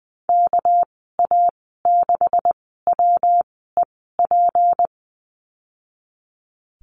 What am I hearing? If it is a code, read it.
Morse code, heard as a keyed tone, about 20 wpm, spaced as for 14 wpm, 700 Hz. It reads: KA6WEP
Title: 238 Lesson 38g - Letter q - Call Signs 20wpm